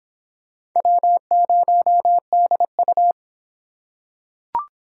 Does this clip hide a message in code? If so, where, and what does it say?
Yes: Morse code from 0.8 to 3.1 s, heard as a keyed tone, about 23 wpm, 700 Hz. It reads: W0DU